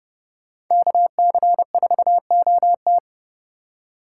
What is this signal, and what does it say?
Morse code, heard as a keyed tone, about 29 wpm, 700 Hz. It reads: KC4OT